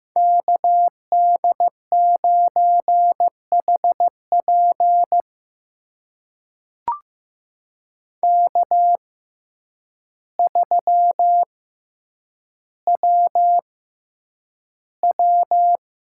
Morse code, heard as a keyed tone, about 15 wpm, spaced as for 8 wpm, 700 Hz. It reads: KD9HP K3WW
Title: USA Call Signs - 6x Character Spacing - 15wpm